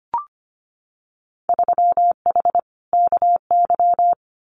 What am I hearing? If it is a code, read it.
Morse code, heard as a keyed tone, about 25 wpm, 700 Hz. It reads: 3H KY
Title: Winter Field Day Exchanges Encore 25wpm